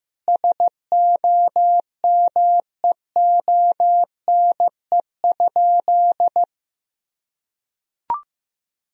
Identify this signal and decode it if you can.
Morse code, heard as a keyed tone, about 16 wpm, 700 Hz. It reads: SOMEONE?
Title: Sentences from Top 1000 Words - Review Part 4 - 15wpm